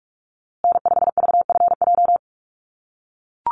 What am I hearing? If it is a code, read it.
Morse code, heard as a keyed tone, about 44 wpm, 700 Hz. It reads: N5VFJ
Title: USA Call Signs Repeated 3x 45wpm